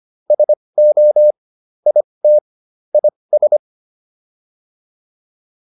Morse code, heard as a keyed tone, about 25 wpm, spaced as for 20 wpm, 600 Hz. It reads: SO IT IS